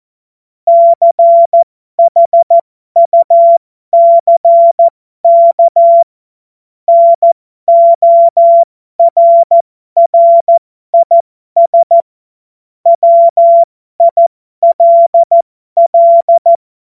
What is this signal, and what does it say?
Morse code, heard as a keyed tone, about 14 wpm, spaced as for 12 wpm, 680 Hz. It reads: CHUCK NORRIS WILL
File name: Morsekurs / Witze